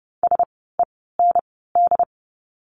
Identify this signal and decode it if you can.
Morse code, heard as a keyed tone, about 30 wpm, 700 Hz. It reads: S E N D
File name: Lesson 16f - Letter c - Words 30wpm